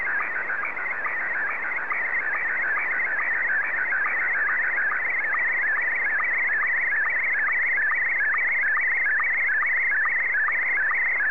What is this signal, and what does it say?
SSTV (slow-scan television), Scottie 1.
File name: sstv / S1-ZL2TJM